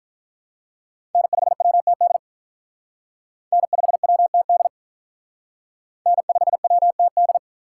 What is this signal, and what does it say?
Morse code, heard as a keyed tone, about 45 wpm, 700 Hz. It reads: N5WTD N5WTD N5WTD